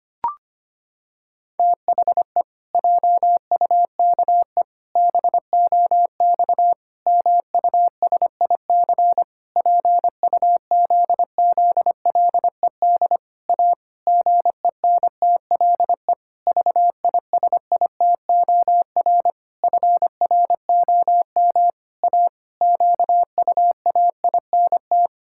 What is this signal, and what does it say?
Morse code, heard as a keyed tone, about 25 wpm, 700 Hz. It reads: THE JUKE BOX MUSIC PUZZLED A GENTLE VISITOR FROM A QUAINT
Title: Pangram Challenge v2 - 25wpm